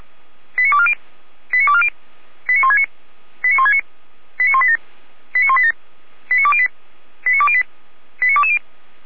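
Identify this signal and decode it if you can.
Five-tone selective call (ZVEI), heard as quick runs of five tones, tones of about 70 ms. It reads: 89240 89240 89169 89169 89187 89187 89208 89208 89220
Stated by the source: ZVEI five-tone call 89230 received by SDR, logged by the Orlamünde fire brigade (feuerwehr-orlamuende).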